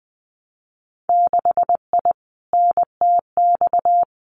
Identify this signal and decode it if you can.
Morse code, heard as a keyed tone, about 20 wpm, 700 Hz. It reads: 6I NTX
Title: Winter Field Day Exchanges 20wpm